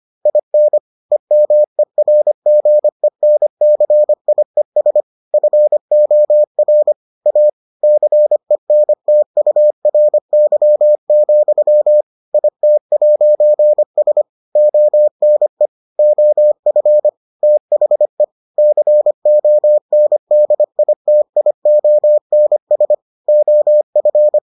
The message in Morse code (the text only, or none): IN EMERGENCIES FOR A CENTURY, IT'S ONE OF THE CONDITIONS OF